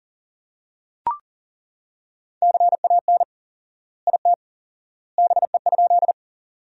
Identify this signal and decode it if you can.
Morse code, heard as a keyed tone, about 40 wpm, 700 Hz. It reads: CAN IT BE?
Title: 4x Word-Spacing - Sentences from Top 100 Words - 40wpm